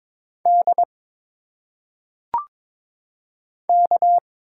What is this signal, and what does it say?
Morse code, heard as a keyed tone, about 22 wpm, 700 Hz. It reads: D K
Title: Single Letters - Phonetic Pronunciation 22wpm